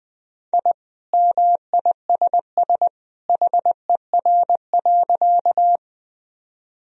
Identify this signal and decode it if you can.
Morse code, heard as a keyed tone, about 19 wpm, 700 Hz. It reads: I MISS HER.
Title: Sentences from Top 1000 Words - Review Part 3 - 20wpm